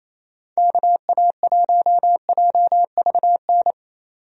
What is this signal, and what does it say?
Morse code, heard as a keyed tone, about 28 wpm, 700 Hz. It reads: KA1JVN
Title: USA Call Signs - 6 Characters - 28wpm